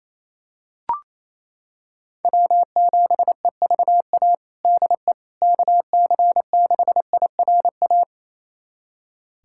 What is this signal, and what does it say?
Morse code, heard as a keyed tone, about 28 wpm, 700 Hz. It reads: W7EVA DE KC6IRA
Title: USA Callsign DE Callsign 28wpm